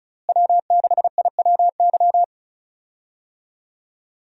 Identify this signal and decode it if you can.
Morse code, heard as a keyed tone, about 36 wpm, 700 Hz. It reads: W6IWY
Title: USA Call Signs - 5 Characters - Repeated 2x - 35wpm